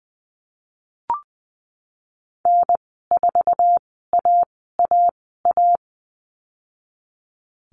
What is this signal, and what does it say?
Morse code, heard as a keyed tone, about 20 wpm, spaced as for 15 wpm, 700 Hz. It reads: N4AAA
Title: 055 Lesson 10g - Letter r - Call Signs 20wpm